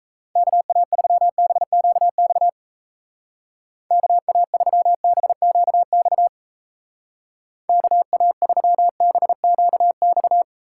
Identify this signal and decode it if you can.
Morse code, heard as a keyed tone, about 36 wpm, 700 Hz. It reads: KA3BQX KA3BQX KA3BQX